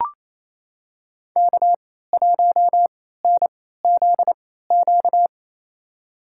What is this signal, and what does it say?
Morse code, heard as a keyed tone, about 28 wpm, spaced as for 16 wpm, 700 Hz. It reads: K1NZQ
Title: USA Call Signs - 3x Character Spacing - 28wpm